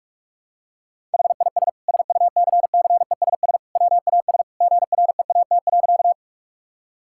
Morse code, heard as a keyed tone, about 41 wpm, 700 Hz. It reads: HIS SUCCESS WAS GREAT.
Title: Sentences from Top 1000 Words - Review Part 6 - 45wpm